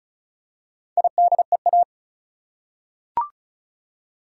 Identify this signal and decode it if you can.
Morse code, heard as a keyed tone, about 34 wpm, 700 Hz. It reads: IDEA
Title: Four Letter Words - 35wpm